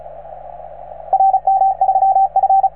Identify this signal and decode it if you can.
Morse code, heard as a keyed tone, about 36 wpm, 770 Hz, with radio noise. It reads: RM3F